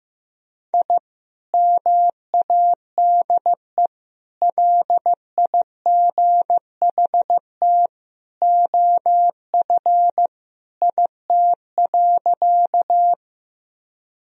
Morse code, heard as a keyed tone, about 15 wpm, 700 Hz. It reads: I MADE LIGHT OF IT.
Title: Sentences from Top 1000 Words - Review Part 1 - 15wpm